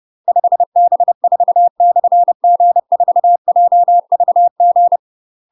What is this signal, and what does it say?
Morse code, heard as a keyed tone, about 30 wpm, 700 Hz. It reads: 5B4/G4JVG
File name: UK MASTER QSOs TRAINING / 30WPM / UK MASTER QSOs 0024